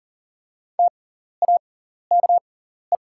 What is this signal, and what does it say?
Morse code, heard as a keyed tone, about 43 wpm, spaced as for 14 wpm, 700 Hz. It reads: TAKE
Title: 6x Character Spacing - Top 200 Words - 40wpm